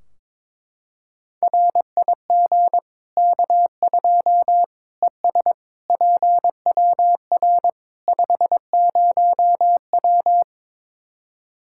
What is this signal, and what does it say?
Morse code, heard as a keyed tone, about 22 wpm, 700 Hz. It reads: RIG K2 ES PWR 50W
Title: Practice QSO - Exchange 2 - Info - 22wpm